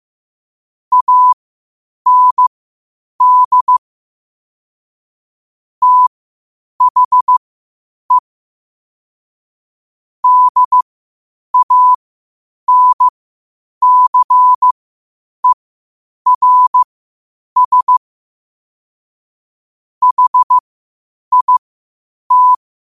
Morse code, heard as a keyed tone, about 15 wpm, spaced as for 8 wpm, 1000 Hz. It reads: AND THE DANCERS HIT
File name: qotdfiles / 8wpm